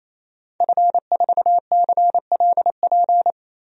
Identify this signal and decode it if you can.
Morse code, heard as a keyed tone, about 27 wpm, 700 Hz. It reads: F4CLP
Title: DXCC-100 Call Signs - 5 Characters - 28wpm